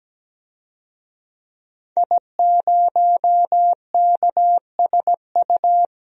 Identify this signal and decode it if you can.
Morse code, heard as a keyed tone, about 17 wpm, 700 Hz. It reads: I0KSU
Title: DXCC-100 Call Signs - 5 Characters - 17wpm